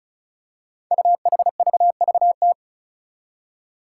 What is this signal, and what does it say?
Morse code, heard as a keyed tone, about 36 wpm, 700 Hz. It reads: UHVVT
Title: Five Challenging Letter-Number Combinations 35wpm